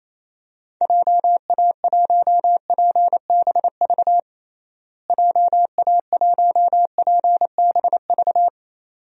Morse code, heard as a keyed tone, about 28 wpm, 700 Hz. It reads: JA1PBV JA1PBV
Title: DXCC-100 Call Signs - 6 Characters - Repeated 2x - 28wpm